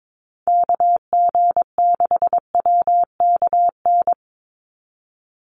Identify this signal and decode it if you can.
Morse code, heard as a keyed tone, about 22 wpm, 700 Hz. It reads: KG6WKN
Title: USA Call Signs 22wpm